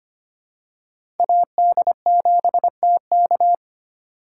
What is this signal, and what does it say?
Morse code, heard as a keyed tone, about 25 wpm, 700 Hz. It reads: AD7TK
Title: USA Call Signs - 5 Characters - 6x Character Spacing - 25wpm